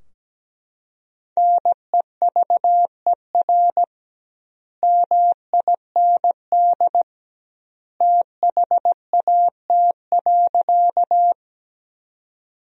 Morse code, heard as a keyed tone, about 17 wpm, 700 Hz. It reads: NEVER MIND THAT.
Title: 2x Word-Spacing - Sentences from Top 500 Words - 17wpm